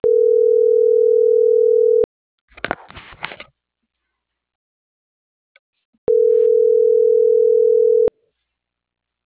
Ringback tone (North America), 440 + 480 Hz.